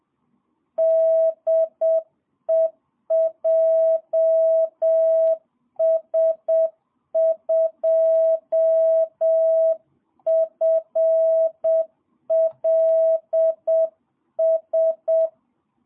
Morse code, heard as a keyed tone, about 7 wpm, 650 Hz. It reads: DE JS2FLS
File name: jh8cbh / 2022cw